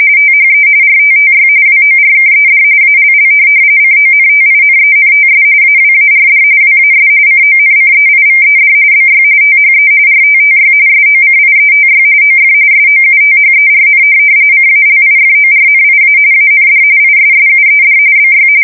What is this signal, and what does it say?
RTTY, about 45.45 baud. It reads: THE RTTY WORKED ALL CONTINENTS AWARD WAS CONCEIVED BY THE RTTY SOCIETY OF SOUTHERN CALIFORNIA AND ISSUED BY
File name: fsk170 / teletype-RTTY